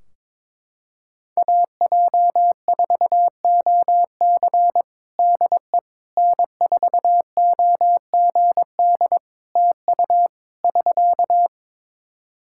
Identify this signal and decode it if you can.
Morse code, heard as a keyed tone, about 22 wpm, 700 Hz. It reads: AJ4OC DE N4OGD TU <SK>
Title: Practice QSO - Exchange 4 - Ending - 22wpm